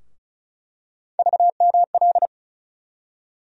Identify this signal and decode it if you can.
Morse code, heard as a keyed tone, about 36 wpm, 700 Hz. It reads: VML
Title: USA Call Sign Postfixes - 3 Letters 35wpm